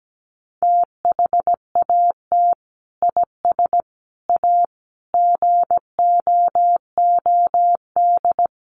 Morse code, heard as a keyed tone, about 17 wpm, 700 Hz. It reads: THAT IS A GOOD
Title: Sentences from Top 900 Words 17wpm